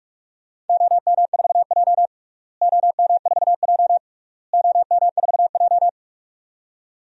Morse code, heard as a keyed tone, about 41 wpm, 700 Hz. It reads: OM4J OM4J OM4J
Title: DXCC-100 Call Signs - 4 Characters - Repeated 3x - 45wpm